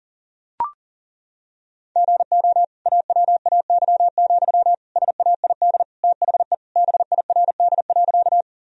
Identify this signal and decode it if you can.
Morse code, heard as a keyed tone, about 40 wpm, 700 Hz. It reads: GO AWAY, SAID THE BIRD.
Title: Sentences from Top 1000 Words - Review Part 2 - 40wpm